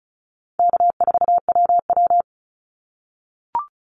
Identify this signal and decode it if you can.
Morse code, heard as a keyed tone, about 34 wpm, 700 Hz. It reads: K4WW
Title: Call Signs - 35wpm